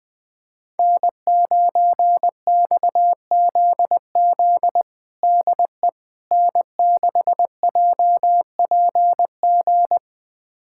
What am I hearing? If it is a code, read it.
Morse code, heard as a keyed tone, about 21 wpm, 700 Hz. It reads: N9XZZ DE N6JPG
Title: USA Callsign DE Callsign - 5 Characters - 20wpm